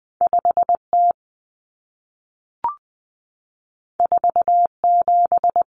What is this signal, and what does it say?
Morse code, heard as a keyed tone, about 20 wpm, 700 Hz. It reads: 5T 47
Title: Common Cut Numbers - 2-Digit 20wpm